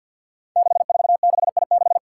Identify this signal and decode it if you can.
Morse code, heard as a keyed tone, about 48 wpm, 700 Hz. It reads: 646I6
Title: Five Challenging Letter-Number Combinations 50wpm